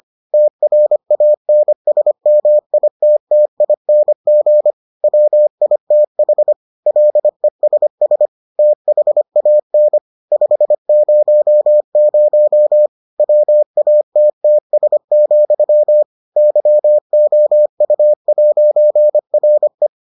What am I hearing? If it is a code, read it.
Morse code, heard as a keyed tone, about 25 wpm, 600 Hz. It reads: TRANSMITTING WITH LESS THAN 500 WATTS, YOU'RE